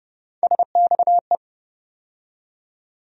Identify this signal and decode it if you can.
Morse code, heard as a keyed tone, about 30 wpm, 700 Hz. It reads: SXE
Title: USA Call Sign Postfixes - 3 Letters 30wpm